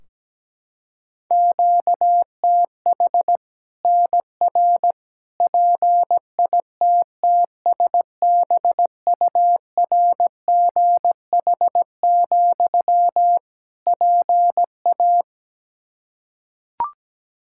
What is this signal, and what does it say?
Morse code, heard as a keyed tone, about 17 wpm, 700 Hz. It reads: QTH NR PITTSBURGH, PA
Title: Practice QSO - Exchange 1 - Beginning - 17wpm